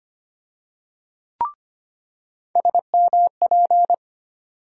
Morse code, heard as a keyed tone, about 25 wpm, 700 Hz. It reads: SMP
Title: USA Call Sign Postfixes - 3 Letters 25wpm